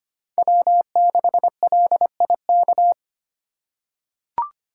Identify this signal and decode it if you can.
Morse code, heard as a keyed tone, about 25 wpm, 700 Hz. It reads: W6LIK
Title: USA Call Signs - 5 Characters - 2x Character Spacing - 25wpm